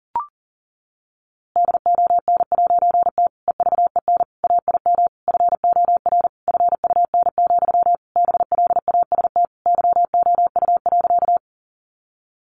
Morse code, heard as a keyed tone, about 40 wpm, 700 Hz. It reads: DON'T EVEN AIM FOR FUN, BLAST YOU.
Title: A Sound Of Thunder 40wpm